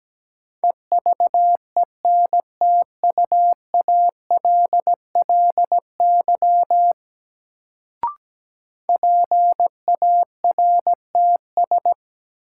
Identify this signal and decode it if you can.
Morse code, heard as a keyed tone, about 17 wpm, 700 Hz. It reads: EVENTUALLY PARTS
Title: Sentences from Top 1000 Words - Review Part 6 - New Word Drill - 17wpm